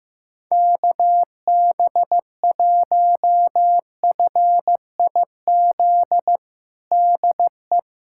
Morse code, heard as a keyed tone, about 15 wpm, 700 Hz. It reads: KB1FIZ DE
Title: USA Callsign DE Callsign 15wpm